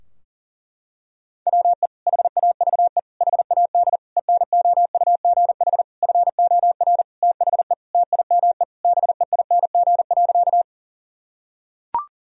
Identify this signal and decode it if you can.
Morse code, heard as a keyed tone, about 40 wpm, 700 Hz. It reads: WE HAVE HAD ENOUGH FOR THE TIME BEING.
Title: Sentences from Top 1000 Words - Review Part 3 - 40wpm